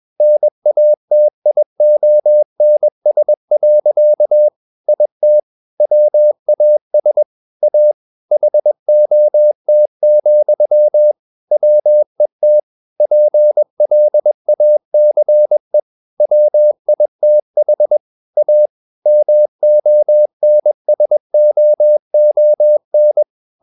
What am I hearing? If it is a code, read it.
Morse code, heard as a keyed tone, about 21 wpm, 600 Hz. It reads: NATIONS. IT WAS A HOT, WET PLACE WITH A MONSOON
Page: CW med Gnister